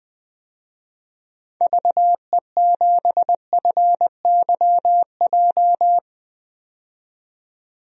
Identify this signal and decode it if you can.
Morse code, heard as a keyed tone, about 20 wpm, 700 Hz. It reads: VE7FYJ
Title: DXCC-100 Call Signs - 6 Characters - 20wpm